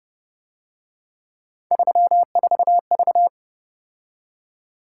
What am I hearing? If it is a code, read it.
Morse code, heard as a keyed tone, about 31 wpm, 700 Hz. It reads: 34V